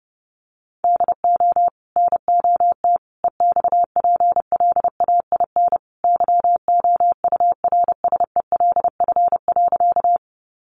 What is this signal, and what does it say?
Morse code, heard as a keyed tone, about 30 wpm, 700 Hz. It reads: DO NOT EXPLAIN YOURSELF.
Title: Sentences from Top 1100 Words 30wpm